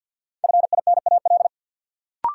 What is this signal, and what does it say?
Morse code, heard as a keyed tone, about 51 wpm, 700 Hz. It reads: FINAL
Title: Sentences from Top 800 Words - New Word Drill - 50wpm